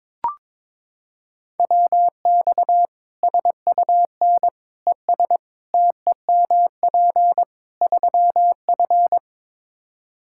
Morse code, heard as a keyed tone, about 22 wpm, 700 Hz. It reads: WX SUN ES TEMP 3F
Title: Practice QSO - Exchange 2 - Info - 22wpm